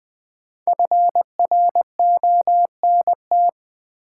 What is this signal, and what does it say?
Morse code, heard as a keyed tone, about 20 wpm, 700 Hz. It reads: FRONT
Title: Sentences from Top 500 Words - New Word Drill - 20wpm